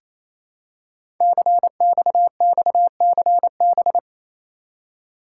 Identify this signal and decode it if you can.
Morse code, heard as a keyed tone, about 29 wpm, 700 Hz. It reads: CXXCB
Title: Code Groups - 5 Letters - 28wpm